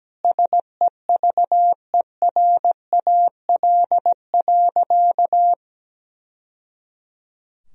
Morse code, heard as a keyed tone, about 17 wpm, 700 Hz. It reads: SEVERAL.
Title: Sentences from Top 1000 Words - Review Part 2 - 17wpm